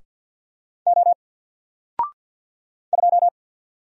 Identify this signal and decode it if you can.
Morse code, heard as a keyed tone, about 54 wpm, 700 Hz. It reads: O 2
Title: Single Character (Letter:Number:Punctuation:Prosign) 50wpm